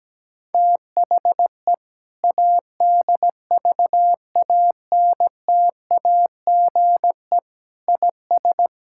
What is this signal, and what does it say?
Morse code, heard as a keyed tone, about 17 wpm, 700 Hz. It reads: THE ADVANTAGE IS
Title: Sentences from Top 1000 Words - Review Part 6 - 17wpm